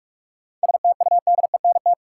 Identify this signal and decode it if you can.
Morse code, heard as a keyed tone, about 46 wpm, 700 Hz. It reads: STUDENT
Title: Sentences from Top 1000 Words - New Word Drill - 45wpm